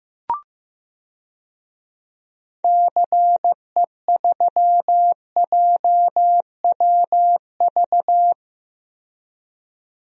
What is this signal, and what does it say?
Morse code, heard as a keyed tone, about 16 wpm, 700 Hz. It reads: CE3JWV
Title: DXCC-100 Call Signs - 6 Characters - 15wpm